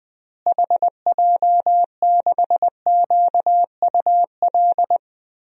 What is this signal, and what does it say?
Morse code, heard as a keyed tone, about 20 wpm, 700 Hz. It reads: HJ6QUL
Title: DXCC-100 Call Signs - 6 Characters - Repeated 3x - 20wpm